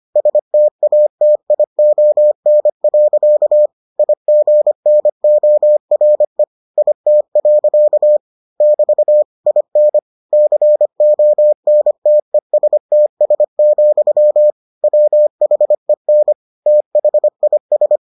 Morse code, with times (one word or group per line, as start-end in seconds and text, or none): 0.2-3.7 STATION.
4.0-6.4 IGNORE
6.8-8.2 IT.
8.6-9.2 <BT>
9.5-10.0 IN
10.3-14.5 CONTESTS,
14.8-16.3 WHEN
16.7-18.0 THIS